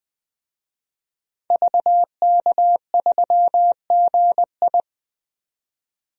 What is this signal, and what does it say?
Morse code, heard as a keyed tone, about 20 wpm, 700 Hz. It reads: VK3GI
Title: DXCC-100 Call Signs - 5 Characters - 20wpm